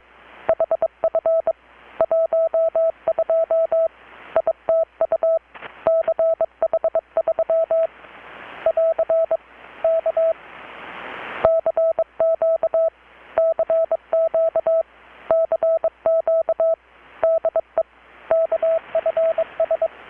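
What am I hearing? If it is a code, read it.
Morse code, heard as a keyed tone, about 22 wpm, 650 Hz. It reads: HF 12 ITU CH3 <AR> K CQ CQ CQ DE KFS
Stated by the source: Morse from KPH's Night of Nights XX maritime radiotelegraphy event, received in Chiapas, Mexico about 3,800 km away on a Kenwood R-600 with a grounded 10 m long-wire antenna.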